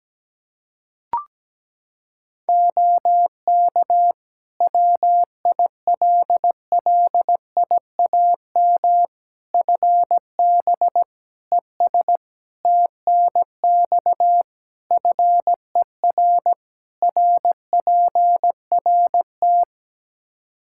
Morse code, heard as a keyed tone, about 17 wpm, 700 Hz. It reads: OK WILLIAM FB ES TNX FER RPRT
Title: Practice QSO - Exchange 2 - Info - 17wpm